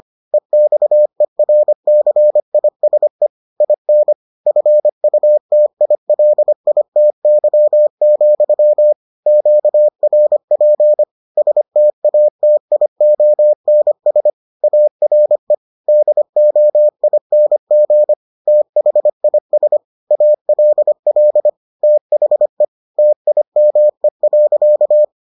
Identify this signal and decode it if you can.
Morse code, heard as a keyed tone, about 25 wpm, 600 Hz. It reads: EXERCISE IN FUTILITY, QRP STATIONS ARE DOING THIS ALL THE TIME.